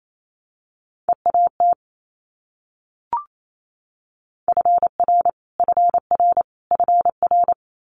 Morse code, heard as a keyed tone, about 28 wpm, 700 Hz. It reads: EAT FR FR FR
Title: Multi-letter Phonemes Repeated 3x 28wpm